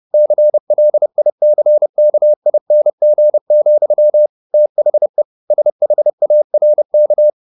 Morse code, heard as a keyed tone, about 30 wpm, 600 Hz. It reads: CLICKING, THE SHARK